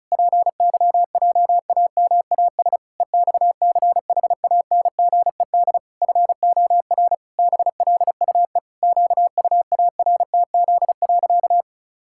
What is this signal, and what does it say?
Morse code, heard as a keyed tone, about 34 wpm, 700 Hz. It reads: PYJAMAS EXCHANGED FOR BLUE QUARTZ.
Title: Pangram Challenge v2 - 35wpm